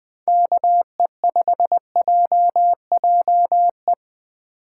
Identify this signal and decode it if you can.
Morse code, heard as a keyed tone, about 21 wpm, 700 Hz. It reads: KE5JJE